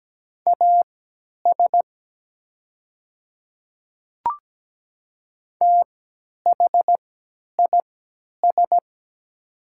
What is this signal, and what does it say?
Morse code, heard as a keyed tone, about 17 wpm, spaced as for 10 wpm, 700 Hz. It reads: AS THIS
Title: 3x Character Spacing - Top 100 Words - 17wpm